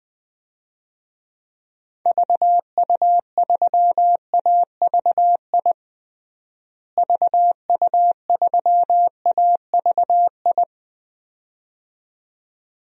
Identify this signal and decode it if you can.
Morse code, heard as a keyed tone, about 20 wpm, 700 Hz. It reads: VU3AVI VU3AVI
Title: DXCC-100 Call Signs - 6 Characters - Repeated 2x - 20wpm